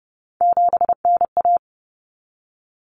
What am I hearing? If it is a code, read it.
Morse code, heard as a keyed tone, about 30 wpm, 700 Hz. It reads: 7NA
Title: Common Cut Numbers - 3-Digit 30wpm